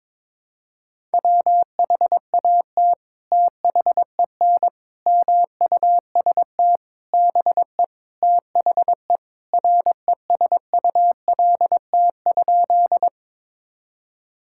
Morse code, heard as a keyed tone, about 22 wpm, 700 Hz. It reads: WHAT THEN MUST BE THE RESULT?
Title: Sentences from Top 1000 Words - Review Part 5 - 22wpm